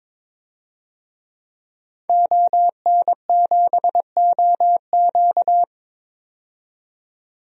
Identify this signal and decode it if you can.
Morse code, heard as a keyed tone, about 22 wpm, 700 Hz. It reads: ON7OQ
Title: DXCC-100 Call Signs - 5 Characters - 22wpm